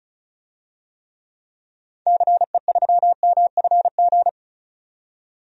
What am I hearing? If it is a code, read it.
Morse code, heard as a keyed tone, about 34 wpm, 700 Hz. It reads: CE3MFG
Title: DXCC-100 Call Signs - 6 Characters - 35wpm